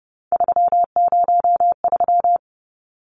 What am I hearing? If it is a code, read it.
Morse code, heard as a keyed tone, about 30 wpm, 700 Hz. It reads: 303
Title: Numbers - 3-Digit 30wpm